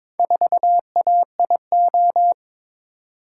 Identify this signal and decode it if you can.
Morse code, heard as a keyed tone, about 22 wpm, 700 Hz. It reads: 4AIO
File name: practice / 22wpm / Four Letter-Number Combinations - 22wpm